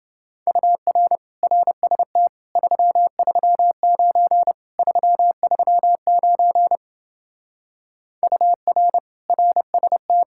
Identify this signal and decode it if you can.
Morse code, heard as a keyed tone, about 28 wpm, 700 Hz. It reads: UR RST 339 339 UR RST